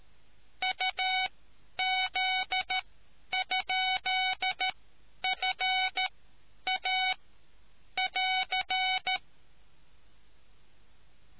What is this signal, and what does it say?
Morse code, heard as a keyed tone, about 13 wpm, spaced as for 9 wpm, 750 Hz. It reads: UZ?FA<AR>